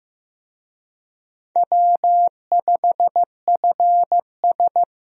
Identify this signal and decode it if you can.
Morse code, heard as a keyed tone, about 15 wpm, 700 Hz. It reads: W5FS
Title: USA Call Signs - 4 Characters - 15wpm